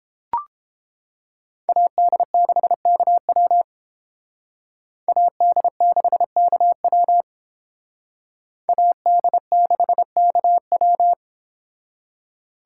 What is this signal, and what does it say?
Morse code, heard as a keyed tone, about 30 wpm, 700 Hz. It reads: AD6KW AD6KW AD6KW